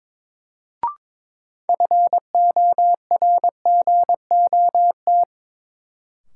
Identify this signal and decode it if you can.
Morse code, heard as a keyed tone, about 22 wpm, 700 Hz. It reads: FORGOT
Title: Sentences from Top 1400 Words - New Word Drill Alt - 22wpm